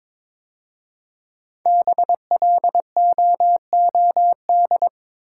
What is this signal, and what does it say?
Morse code, heard as a keyed tone, about 22 wpm, 700 Hz. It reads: BLOOD
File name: 5 Letter Words - First 3 Letters Given - 22wpm